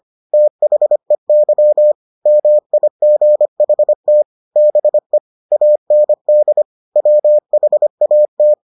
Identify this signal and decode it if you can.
Morse code, heard as a keyed tone, about 25 wpm, 600 Hz. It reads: THEY MIGHT BE AND WHAT